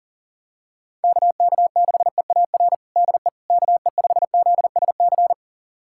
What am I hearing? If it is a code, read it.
Morse code, heard as a keyed tone, about 37 wpm, 700 Hz. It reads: KK6EAR DE KE5ZSC
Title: USA Callsign DE Callsign 40wpm